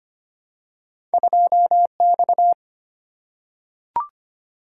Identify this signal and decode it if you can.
Morse code, heard as a keyed tone, about 25 wpm, 700 Hz. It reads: 2X
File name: Two Letter-Number Combinations - 25wpm